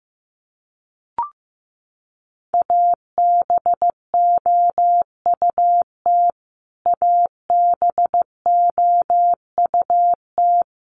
Morse code, heard as a keyed tone, about 15 wpm, 700 Hz. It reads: ABOUT ABOUT
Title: Top 200 Words Repeated 15wpm